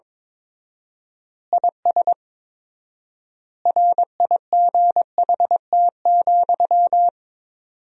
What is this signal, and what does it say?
Morse code, heard as a keyed tone, about 22 wpm, 700 Hz. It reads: IS RIGHT,
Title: 4x Word-Spacing - Sentences from Top 200 Words - 22wpm